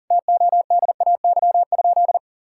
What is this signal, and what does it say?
Morse code, heard as a keyed tone, about 40 wpm, 700 Hz. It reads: TODAY?
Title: Sentences from Top 1000 Words - Review Part 6 - 40wpm